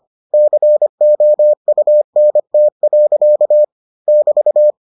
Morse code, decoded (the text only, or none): COUNT. <BT>